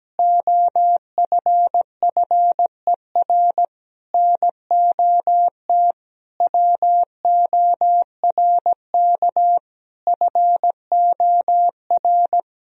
Morse code, heard as a keyed tone, about 17 wpm, 700 Hz. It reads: OFFER NOT WORK FOR
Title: Sentences from Top 1000 Words - Review Part 4 - 17wpm